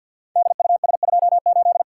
Morse code, heard as a keyed tone, about 53 wpm, 700 Hz. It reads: DUS28
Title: Code Groups - 5 Letter:Numbers - 50wpm